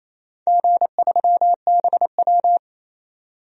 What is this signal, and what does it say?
Morse code, heard as a keyed tone, about 28 wpm, 700 Hz. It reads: G3BW
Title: DXCC-100 Call Signs - 4 Characters - 28wpm